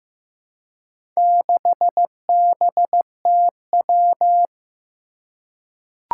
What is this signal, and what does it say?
Morse code, heard as a keyed tone, about 15 wpm, 700 Hz. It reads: 6BTW
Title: Four Challenging Letter-Number Combinations 15wpm